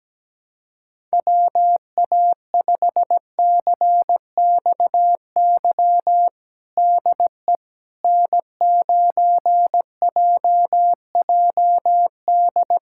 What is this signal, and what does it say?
Morse code, heard as a keyed tone, about 17 wpm, 700 Hz. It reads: WA5CXY DE N9JJD
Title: USA Callsign DE Callsign 17wpm